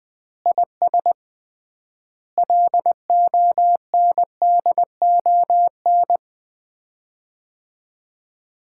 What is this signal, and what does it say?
Morse code, heard as a keyed tone, about 20 wpm, 700 Hz. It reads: IS LONDON
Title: Common QSO Phrases - 3x Word-Spacing - 20wpm